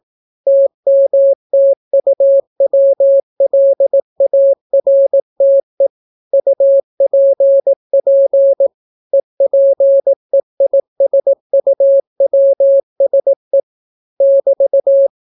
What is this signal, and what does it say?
Morse code, heard as a keyed tone, about 18 wpm, 550 Hz. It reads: TMTUWLARTE UPP EPEISUWSE <BT>